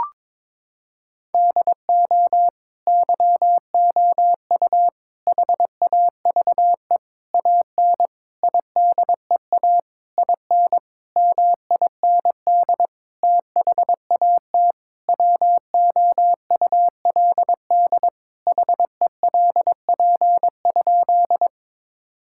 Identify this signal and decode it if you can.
Morse code, heard as a keyed tone, about 22 wpm, 700 Hz. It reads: DO YOU HAVE AN IDEA IN MIND THAT WOULD HELP?
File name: Sentences with 9-Word Phrases - 22wpm